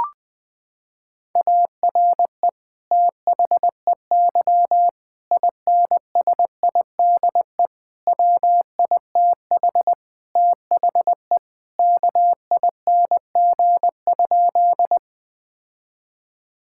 Morse code, heard as a keyed tone, about 20 wpm, 700 Hz. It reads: ARE THEY INSIDE WITH THE KING?
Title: Sentences from Top 1000 Words - Review Part 6 - 20wpm